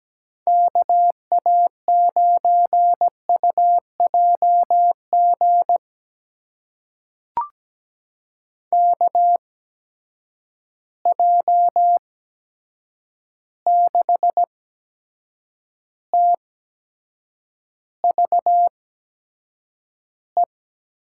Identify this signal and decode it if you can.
Morse code, heard as a keyed tone, about 17 wpm, spaced as for 7 wpm, 700 Hz. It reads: KA9UJG KJ6TVE
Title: USA Call Signs - 6 Characters - 8x Character Spacing - 17wpm